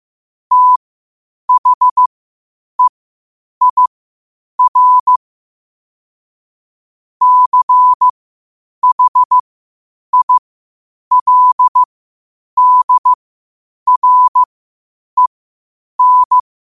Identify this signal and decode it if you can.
Morse code, heard as a keyed tone, about 15 wpm, spaced as for 8 wpm, 1000 Hz. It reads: THEIR CHILDREN
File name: qotdfiles / 8wpm